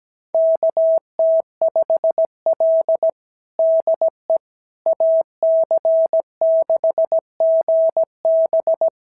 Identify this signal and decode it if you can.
Morse code, heard as a keyed tone, about 15 wpm, 650 Hz. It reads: KT5L DE AC6GB
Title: SOTA QSO Practice 17wpm